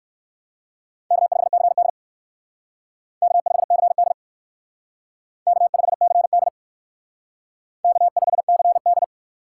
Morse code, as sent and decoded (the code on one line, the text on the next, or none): -.- ..... -.-. -.. / -.- ..... -.-. -.. / -.- ..... -.-. -.. / -.- ..... -.-. -..
K5CD K5CD K5CD K5CD